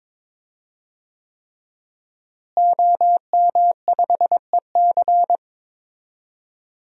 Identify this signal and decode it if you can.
Morse code, heard as a keyed tone, about 22 wpm, 700 Hz. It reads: OM5EC
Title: DXCC-100 Call Signs - 5 Characters - Repeated 3x - 22wpm